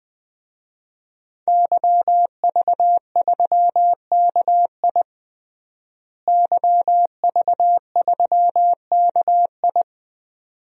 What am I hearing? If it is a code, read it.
Morse code, heard as a keyed tone, about 21 wpm, 700 Hz. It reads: YV3KI YV3KI
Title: DXCC-100 Call Signs - 5 Characters - Repeated 2x - 20wpm